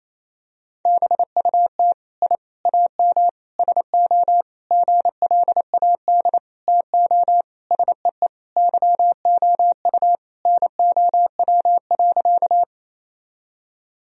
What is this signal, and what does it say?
Morse code, heard as a keyed tone, about 27 wpm, 700 Hz. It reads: BUT I AM SO GLAD TO SEE YOU NOW.
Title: Sentences from Top 1000 Words - Review Part 2 - 28wpm